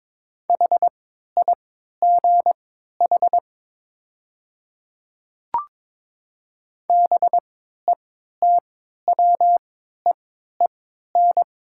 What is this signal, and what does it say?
Morse code, heard as a keyed tone, about 22 wpm, spaced as for 13 wpm, 700 Hz. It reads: HIGH BETWEEN
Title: 3x Character Spacing - Top 300 Words - 22wpm